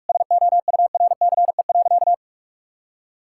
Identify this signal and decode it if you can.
Morse code, heard as a keyed tone, about 46 wpm, 700 Hz. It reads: SOURCE.
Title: Sentences from Top 1000 Words - Review Part 5 - 45wpm